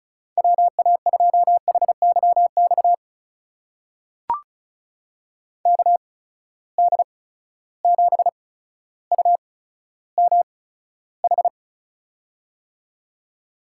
Morse code, heard as a keyed tone, about 36 wpm, spaced as for 15 wpm, 700 Hz. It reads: WA2HYX KD7UMH
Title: USA Call Signs - 6 Characters - 8x Character Spacing - 35wpm